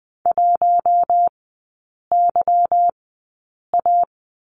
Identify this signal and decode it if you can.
Morse code, heard as a keyed tone, about 20 wpm, spaced as for 8 wpm, 700 Hz. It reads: 1YA
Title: 191 Lesson 31e - Letter v - Three Characters 20wpm